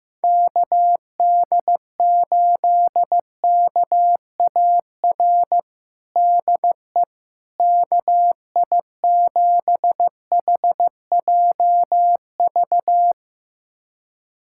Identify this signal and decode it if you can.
Morse code, heard as a keyed tone, about 15 wpm, 700 Hz. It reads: KD8KAR DE KI7HJV
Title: USA Callsign DE Callsign - 6 Characters - 15wpm